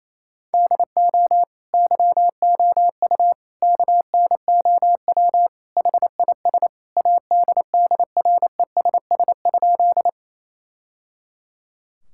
Morse code, heard as a keyed tone, about 28 wpm, 700 Hz. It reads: DO YOU KNOW HIS ADDRESS?
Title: Sentences from Top 1000 Words - Review Part 6 - 28wpm